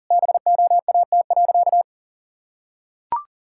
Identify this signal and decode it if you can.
Morse code, heard as a keyed tone, about 37 wpm, 700 Hz. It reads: BOAT.
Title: Sentences from Top 1000 Words - Review Part 6 - 40wpm